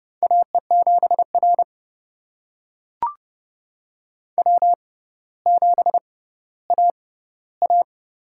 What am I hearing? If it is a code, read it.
Morse code, heard as a keyed tone, about 30 wpm, spaced as for 15 wpm, 700 Hz. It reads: AE7R W7AA
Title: USA Call Signs - 6x Character Spacing - 30wpm